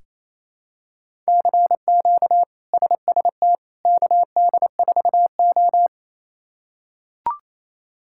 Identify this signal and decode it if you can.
Morse code, heard as a keyed tone, about 28 wpm, 700 Hz. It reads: CQ SST KD4O
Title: K1USN SST QSO Practice - 28wpm